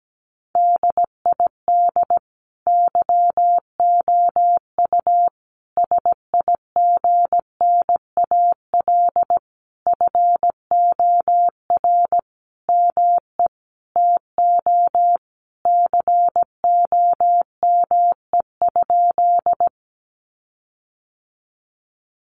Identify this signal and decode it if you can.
Morse code, heard as a keyed tone, about 17 wpm, 700 Hz. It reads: DID YOU SIGNAL FOR ME TO COME?
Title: Sentences from Top 1400 Words 17wpm